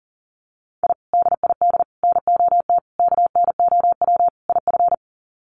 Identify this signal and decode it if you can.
Morse code, heard as a keyed tone, about 40 wpm, 700 Hz. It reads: I DID NOT KNOW IF
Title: Sets of 5 Words Encore 40wpm